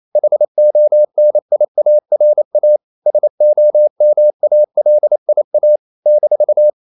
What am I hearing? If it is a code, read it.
Morse code, heard as a keyed tone, about 28 wpm, 600 Hz. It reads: HONIARA SOMALIA -
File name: CW / Practice / NationalCapitals28